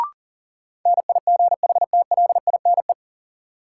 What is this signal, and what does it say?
Morse code, heard as a keyed tone, about 40 wpm, 700 Hz. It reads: NIGHTLINE
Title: Sets of 5 Words - New Word Drill - 40wpm